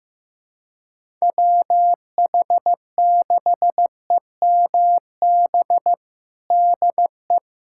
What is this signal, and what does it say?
Morse code, heard as a keyed tone, about 15 wpm, 700 Hz. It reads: WH6EMB DE
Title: USA Callsign DE Callsign 15wpm